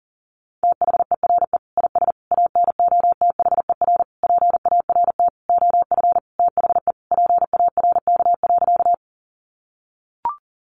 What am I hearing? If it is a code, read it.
Morse code, heard as a keyed tone, about 37 wpm, 700 Hz. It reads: THERE IS ANOTHER PART OF THE PARK.
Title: Sentences from Top 1300 Words 40wpm